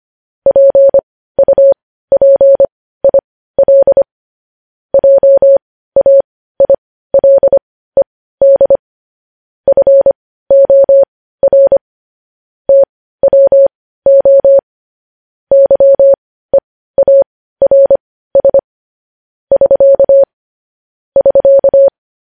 Morse code, heard as a keyed tone, about 25 wpm, spaced as for 15 wpm, 550 Hz. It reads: PUPIL JAILED FOR TWO YEARS <SK> <SK>
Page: news-headlines-in-morse-code-at-15-wpm